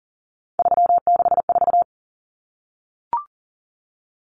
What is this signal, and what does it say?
Morse code, heard as a keyed tone, about 40 wpm, 700 Hz. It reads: 364
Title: Common Cut Numbers - 3-Digit 40wpm